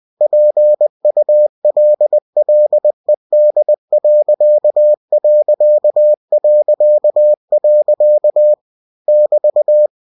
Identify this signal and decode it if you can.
Morse code, heard as a keyed tone, about 20 wpm, 600 Hz. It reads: PULLED.... <BT>
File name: cw-books / winnie-the-pooh / winnie-20wpm-page-0017